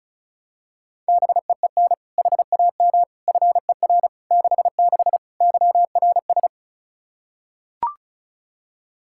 Morse code, heard as a keyed tone, about 34 wpm, 700 Hz. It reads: BEEN HAM FER 66 YRS
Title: Practice QSO - Exchange 3 - More Info - 35wpm